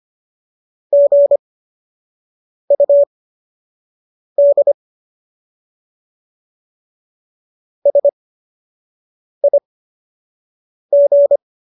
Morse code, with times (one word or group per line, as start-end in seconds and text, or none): 0.9-4.7 GUD
7.9-11.4 SIG